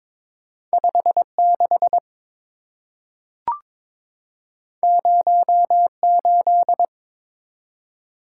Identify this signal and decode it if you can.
Morse code, heard as a keyed tone, about 23 wpm, 700 Hz. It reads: 56 08
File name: practice / 22wpm / Two Letter-Number Combinations - 22wpm